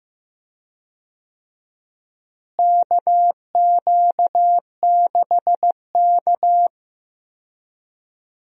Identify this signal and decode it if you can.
Morse code, heard as a keyed tone, about 15 wpm, 700 Hz. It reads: KQ6K
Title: USA Call Signs - 4 Characters - Repeated 2x - 15wpm